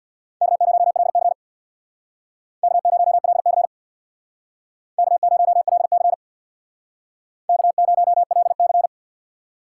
Morse code, heard as a keyed tone, about 63 wpm, 700 Hz. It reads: K0LC K0LC K0LC K0LC